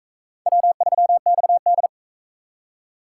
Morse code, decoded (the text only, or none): W3XD